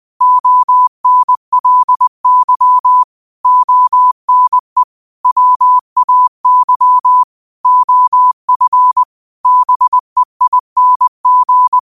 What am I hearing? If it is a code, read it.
Morse code, heard as a keyed tone, about 20 wpm, 1000 Hz. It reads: ONLY ONE WAY OF BEING